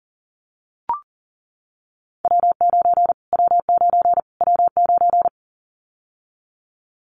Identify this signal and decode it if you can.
Morse code, heard as a keyed tone, about 40 wpm, 700 Hz. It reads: W9 W9 W9
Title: Callsign Prefixes for Top 25 DXCC - Repeated 3x - 40wpm